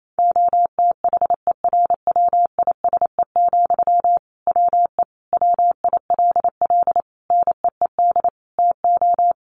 Morse code, heard as a keyed tone, about 28 wpm, 700 Hz. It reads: OTHERWISE, WE WILL NEED TO
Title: Sentences from Top 1500 Words 28wpm